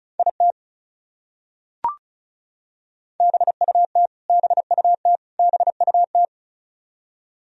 Morse code, heard as a keyed tone, about 34 wpm, 700 Hz. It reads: IT BUT BUT BUT
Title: Top 100 Words Repeated 35wpm